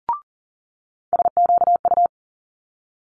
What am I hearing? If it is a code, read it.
Morse code, heard as a keyed tone, about 41 wpm, 700 Hz. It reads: SQU
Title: Multi-letter Phonemes Quick Fire 40wpm